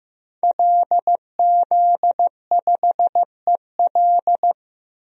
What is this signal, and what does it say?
Morse code, heard as a keyed tone, about 15 wpm, 700 Hz. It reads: LZ5EL
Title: DXCC-100 Call Signs - 5 Characters - 15wpm